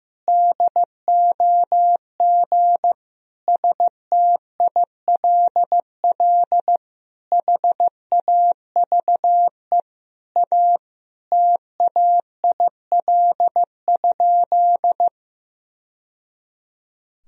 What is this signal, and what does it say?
Morse code, heard as a keyed tone, about 15 wpm, 700 Hz. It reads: DOG STILL HAVE A TAIL?
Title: Sentences from Top 1000 Words - Review Part 3 - 15wpm